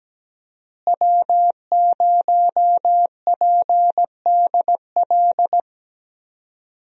Morse code, heard as a keyed tone, about 17 wpm, 700 Hz. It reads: W0PDL
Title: USA Call Signs - 5 Characters - Repeated 3x - 17wpm